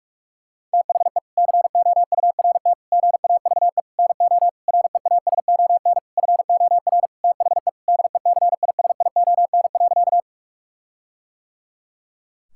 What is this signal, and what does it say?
Morse code, heard as a keyed tone, about 46 wpm, 700 Hz. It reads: THE COURT GAVE NO REASON FOR THE DECISION.
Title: Sentences from Top 1600 Words 45wpm